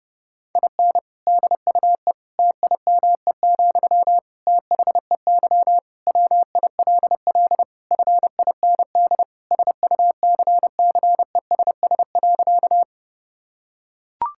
Morse code, heard as a keyed tone, about 32 wpm, 700 Hz. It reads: IN DUE TIME, THEY WILL FIND SUCCESS.
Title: Sentences from Top 1000 Words - Review Part 6 - 30wpm